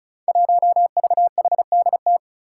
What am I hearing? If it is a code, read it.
Morse code, heard as a keyed tone, about 34 wpm, 700 Hz. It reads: 1VHDT
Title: Five Challenging Letter-Number Combinations 35wpm